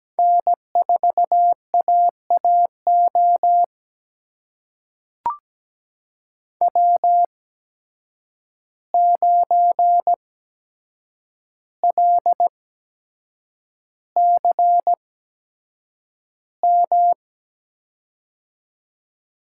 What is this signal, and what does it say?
Morse code, heard as a keyed tone, about 17 wpm, spaced as for 7 wpm, 700 Hz. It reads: N4AAO W9LCM
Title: USA Call Signs - 5 Characters - 8x Character Spacing - 17wpm